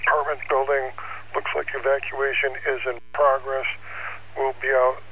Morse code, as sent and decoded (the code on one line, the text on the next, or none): none